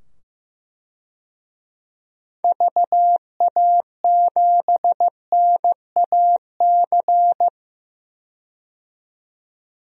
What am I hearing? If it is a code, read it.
Morse code, heard as a keyed tone, about 15 wpm, 700 Hz. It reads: VA7NAC
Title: DXCC-100 Call Signs - 6 Characters - 15wpm